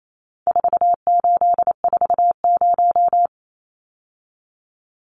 Morse code, heard as a keyed tone, about 28 wpm, 700 Hz. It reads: 4840
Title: Numbers - 4-Digit 28wpm